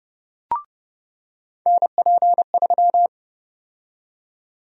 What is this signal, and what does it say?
Morse code, heard as a keyed tone, about 30 wpm, 700 Hz. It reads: NP3
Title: USA Call Sign Prefixes - 30wpm